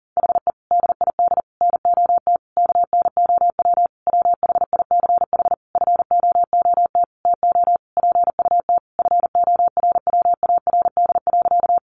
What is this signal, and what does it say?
Morse code, heard as a keyed tone, about 37 wpm, 700 Hz. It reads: HE DID NOT KNOW WHICH FOOT TO PUT FORWARD.
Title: Sentences from Top 1300 Words 40wpm